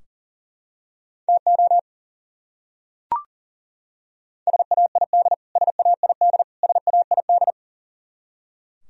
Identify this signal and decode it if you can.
Morse code, heard as a keyed tone, about 37 wpm, 700 Hz. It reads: TO SAID SAID SAID
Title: Top 100 Words Repeated 40wpm